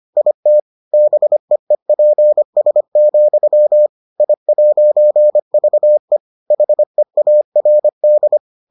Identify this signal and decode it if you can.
Morse code, heard as a keyed tone, about 25 wpm, 600 Hz. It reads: IT BEEPS, I'VE HEARD